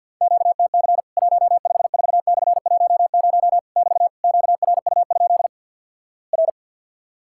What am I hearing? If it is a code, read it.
Morse code, heard as a keyed tone, about 50 wpm, 700 Hz. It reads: QTC 154/10 <BT> QRU? R
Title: Worked All Europe DX Contest - QTC Exchange - 50wpm